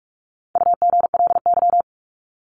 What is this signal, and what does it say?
Morse code, heard as a keyed tone, about 44 wpm, 700 Hz. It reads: UGLY
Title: Top 1751-2000 Words 45wpm